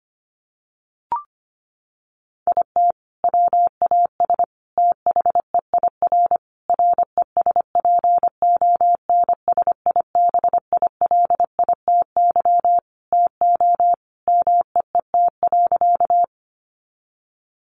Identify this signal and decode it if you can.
Morse code, heard as a keyed tone, about 25 wpm, 700 Hz. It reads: IT WAS THEIR RESPONSIBILITY TO MEET.
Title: Sentences from Top 1200 Words 25wpm